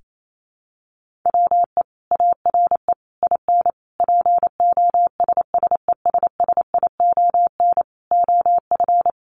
Morse code, heard as a keyed tone, about 28 wpm, 700 Hz. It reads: WE ARE IN POSSESSION OF
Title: Sentences from Top 1400 Words 28wpm